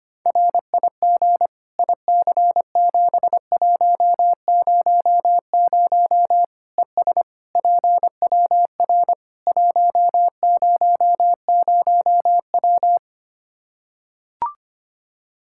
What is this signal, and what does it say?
Morse code, heard as a keyed tone, about 25 wpm, 700 Hz. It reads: RIG IC7100 ES PWR 100W
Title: Practice QSO - Exchange 2 - Info - 25wpm